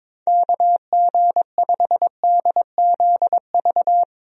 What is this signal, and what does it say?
Morse code, heard as a keyed tone, about 22 wpm, 700 Hz. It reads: KG5DZV
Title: USA Call Signs - 6 Characters - 22wpm